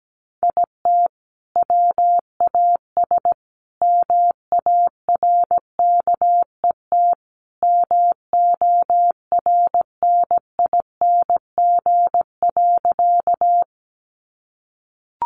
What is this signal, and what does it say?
Morse code, heard as a keyed tone, about 17 wpm, 700 Hz. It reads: IT WAS MARKET MORNING.
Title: Sentences from Top 600 Words 17wpm